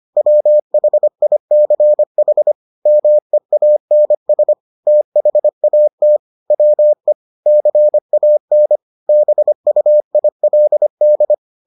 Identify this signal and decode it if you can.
Morse code, heard as a keyed tone, about 25 wpm, 600 Hz. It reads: WHICH MEANS THAT WE CAN BUILD